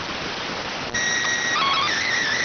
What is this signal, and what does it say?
SSTV (slow-scan television), PD 290.